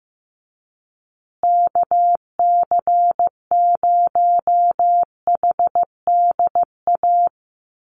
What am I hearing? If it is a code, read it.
Morse code, heard as a keyed tone, about 15 wpm, 700 Hz. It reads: KC0HDA
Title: USA Call Signs Repeated 2x 15wpm